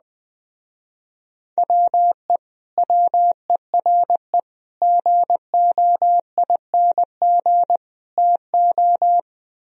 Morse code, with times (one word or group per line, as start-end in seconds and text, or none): 1.6-2.4 WE
2.8-4.4 WERE
4.8-7.8 GOING
8.2-9.2 TO